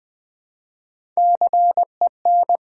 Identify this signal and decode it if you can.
Morse code, heard as a keyed tone, about 20 wpm, 700 Hz. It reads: CEN
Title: Build A Word - 1 Letter At A Time - 4-Letter Words - 20wpm